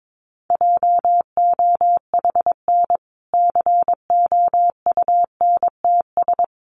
Morse code, heard as a keyed tone, about 22 wpm, 700 Hz. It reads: JOHN COUNTS
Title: Easy Phrases 22wpm